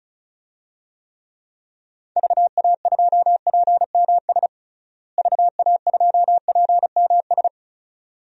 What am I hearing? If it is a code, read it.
Morse code, heard as a keyed tone, about 34 wpm, 700 Hz. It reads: VA2PMS VA2PMS
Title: DXCC-100 Call Signs - 6 Characters - Repeated 2x - 35wpm